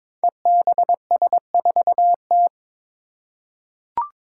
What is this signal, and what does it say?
Morse code, heard as a keyed tone, about 22 wpm, 700 Hz. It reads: EBS4T
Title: Five Challenging Letter-Number Combinations 22wpm